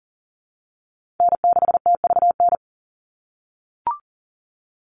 Morse code, heard as a keyed tone, about 37 wpm, 700 Hz. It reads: N6TVN
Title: Call Signs - 40wpm